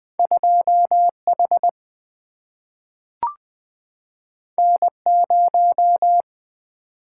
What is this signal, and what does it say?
Morse code, heard as a keyed tone, about 20 wpm, 700 Hz. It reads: 2H N0
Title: Single-Letter Single-Number 20wpm